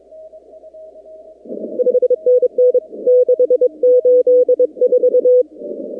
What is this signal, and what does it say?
Morse code, heard as a keyed tone, about 22 wpm, 500 Hz. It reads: 5NN 684
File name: contests / 2019-cw-wpx-kt5j-k5pi- -2330 0-45